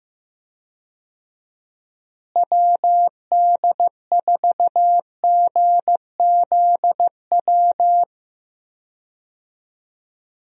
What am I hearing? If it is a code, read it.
Morse code, heard as a keyed tone, about 15 wpm, 700 Hz. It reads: WD4GZW
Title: USA Call Signs - 6 Characters - Repeated 3x - 15wpm